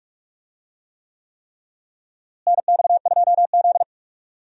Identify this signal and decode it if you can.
Morse code, heard as a keyed tone, about 46 wpm, 700 Hz. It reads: NX2Z
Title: USA Call Signs - 4 Characters - Repeated 3x - 45wpm